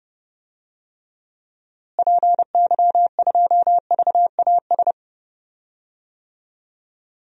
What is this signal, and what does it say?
Morse code, heard as a keyed tone, about 32 wpm, 700 Hz. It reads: PY2VAS
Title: DXCC-100 Call Signs - 6 Characters - Repeated 2x - 30wpm